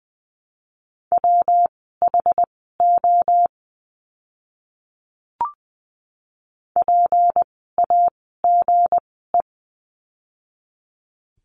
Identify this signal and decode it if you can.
Morse code, heard as a keyed tone, about 20 wpm, spaced as for 15 wpm, 700 Hz. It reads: WHO PAGE
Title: 160 Lesson 26f - Letter g - Words 20wpm